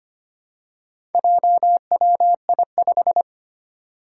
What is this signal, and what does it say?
Morse code, heard as a keyed tone, about 25 wpm, 700 Hz. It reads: JWI5
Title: Four Challenging Letter-Number Combinations 25wpm